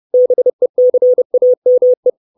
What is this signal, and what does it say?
Morse code, heard as a keyed tone, about 30 wpm, 500 Hz. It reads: BECAME
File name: 632 Words 30WPM